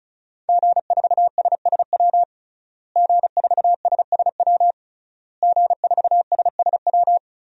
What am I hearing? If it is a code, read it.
Morse code, heard as a keyed tone, about 36 wpm, 700 Hz. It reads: G4SSW G4SSW G4SSW
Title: DXCC-100 Call Signs - 5 Characters - Repeated 3x - 35wpm